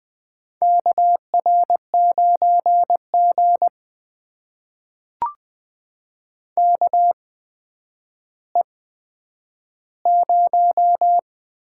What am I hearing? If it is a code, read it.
Morse code, heard as a keyed tone, about 20 wpm, spaced as for 10 wpm, 700 Hz. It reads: KR9G KE0
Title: USA Call Signs - 8x Character Spacing - 20wpm